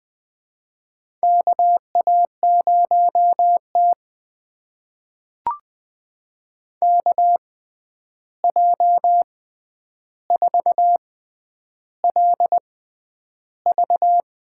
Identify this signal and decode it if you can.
Morse code, heard as a keyed tone, about 20 wpm, spaced as for 10 wpm, 700 Hz. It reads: KA0T KJ4LV
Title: USA Call Signs - 6x Character Spacing - 20wpm